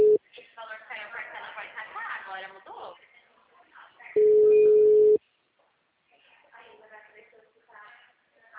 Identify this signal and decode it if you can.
Ringback tone (Europe, 425 Hz), with speech.